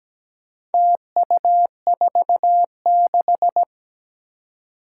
Morse code, heard as a keyed tone, about 17 wpm, 700 Hz. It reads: TU46
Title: Four Challenging Letter-Number Combinations 17wpm